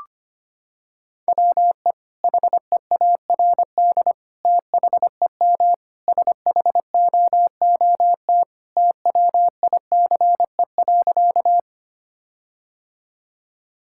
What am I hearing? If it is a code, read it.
Morse code, heard as a keyed tone, about 25 wpm, 700 Hz. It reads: WE HEARD THEM SHOOT TWICE.
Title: Sentences from Top 1600 Words 25wpm